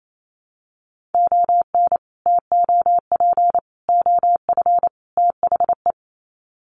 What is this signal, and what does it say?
Morse code, heard as a keyed tone, about 28 wpm, 700 Hz. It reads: ON TOP OF THE
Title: Sets of 4 Words 28wpm